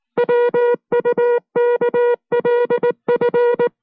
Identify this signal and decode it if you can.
Morse code, heard as a keyed tone, about 19 wpm, 480 Hz. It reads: WUKLF